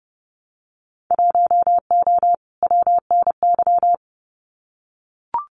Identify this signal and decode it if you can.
Morse code, heard as a keyed tone, about 30 wpm, 700 Hz. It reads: 1O WNY
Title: Winter Field Day Exchanges 30wpm